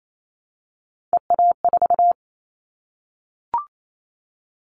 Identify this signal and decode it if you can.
Morse code, heard as a keyed tone, about 28 wpm, 700 Hz. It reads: EA4
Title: Callsign Prefixes for Top 25 DXCC - 28wpm